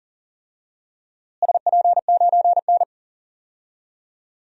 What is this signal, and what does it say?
Morse code, heard as a keyed tone, about 38 wpm, 700 Hz. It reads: SP9N
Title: DXCC-100 Call Signs - 4 Characters - Repeated 3x - 40wpm